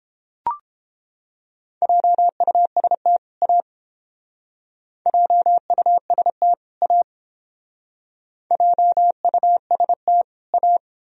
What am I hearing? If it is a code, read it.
Morse code, heard as a keyed tone, about 30 wpm, 700 Hz. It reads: JUST A JUST A JUST A